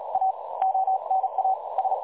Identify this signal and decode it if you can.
Morse code, heard as a keyed tone, about 38 wpm, 760 Hz, with noise. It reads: A ONA S
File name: dlazaawansowanych / nagraniazpasma / websdr recording 3565.0kHz